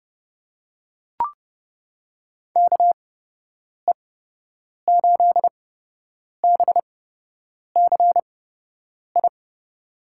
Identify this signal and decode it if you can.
Morse code, heard as a keyed tone, about 31 wpm, spaced as for 8 wpm, 700 Hz. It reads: KE8BCI
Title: USA Call Signs - 6 Characters - 8x Character Spacing - 30wpm